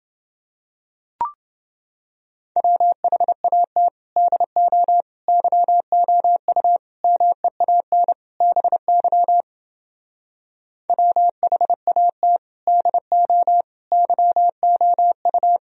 Morse code, heard as a keyed tone, about 29 wpm, 700 Hz. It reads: WHAT DO YOU MEAN BY WHAT DO YOU